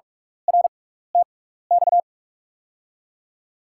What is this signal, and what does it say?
Morse code, heard as a keyed tone, about 42 wpm, spaced as for 15 wpm, 700 Hz. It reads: RTX
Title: USA Call Signs - 5 Characters - 6x Character Spacing - 45wpm